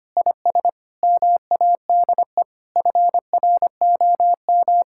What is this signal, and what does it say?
Morse code, heard as a keyed tone, about 25 wpm, 700 Hz. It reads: IS MADE FROM
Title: Sentences from Top 1000 Words - Review Part 6 - 25wpm